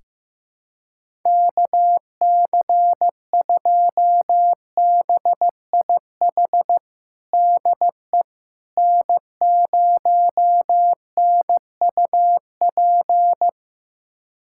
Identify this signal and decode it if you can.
Morse code, heard as a keyed tone, about 15 wpm, 700 Hz. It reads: KC2BIH DE N0NUP
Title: USA Callsign DE Callsign 15wpm